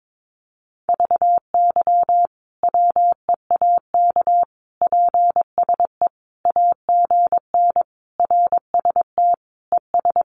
Morse code, heard as a keyed tone, about 22 wpm, 700 Hz. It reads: VY WEAK PSE AGN RST ES
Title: Common QSO Phrases 22wpm